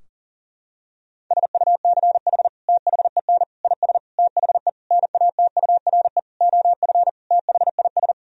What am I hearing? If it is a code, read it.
Morse code, heard as a keyed tone, about 40 wpm, 700 Hz. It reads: SUCH THEN IS THE NATURE OF THIS
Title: Sentences from Top 1000 Words - Review Part 4 - 40wpm